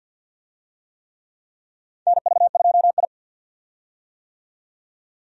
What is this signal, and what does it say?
Morse code, heard as a keyed tone, about 50 wpm, 700 Hz. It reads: NV2I
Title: USA Call Signs - 4 Characters - 50wpm